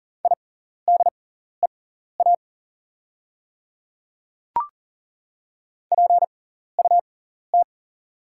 Morse code, heard as a keyed tone, about 41 wpm, spaced as for 14 wpm, 690 Hz. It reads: IDEA PUT
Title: 6x Character Spacing - Top 300 Words - 40wpm